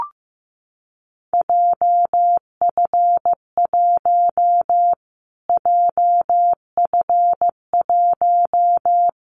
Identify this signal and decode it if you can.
Morse code, heard as a keyed tone, about 15 wpm, 700 Hz. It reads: JF1 JF1
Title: Callsign Prefixes for Top 25 DXCC - Repeated 3x - 15wpm